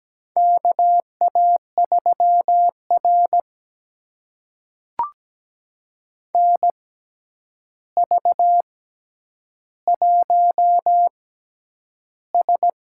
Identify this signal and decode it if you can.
Morse code, heard as a keyed tone, about 17 wpm, spaced as for 9 wpm, 700 Hz. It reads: KA3R NV1S
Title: USA Call Signs - 4 Characters - 6x Character Spacing - 17wpm